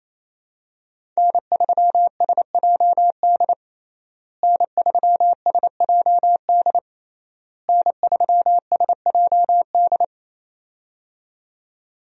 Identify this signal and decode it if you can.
Morse code, heard as a keyed tone, about 28 wpm, 700 Hz. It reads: N3SJD N3SJD N3SJD
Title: USA Call Signs - 5 Characters - Repeated 3x - 28wpm